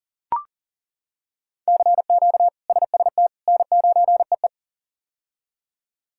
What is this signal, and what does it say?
Morse code, heard as a keyed tone, about 40 wpm, 700 Hz. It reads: CQ SST N9EE
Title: K1USN SST QSO Practice - 40wpm